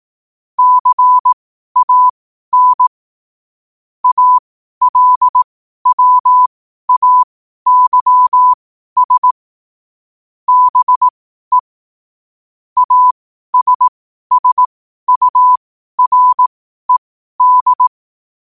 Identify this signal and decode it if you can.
Morse code, heard as a keyed tone, about 18 wpm, spaced as for 12 wpm, 1000 Hz. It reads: CAN ALWAYS BE ASSURED
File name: qotdfiles / 12wpm